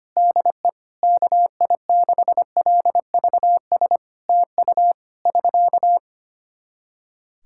Morse code, heard as a keyed tone, about 25 wpm, 700 Hz. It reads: DE KI6LVS TU <SK>
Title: Practice QSO - Exchange 4 - Ending - 25wpm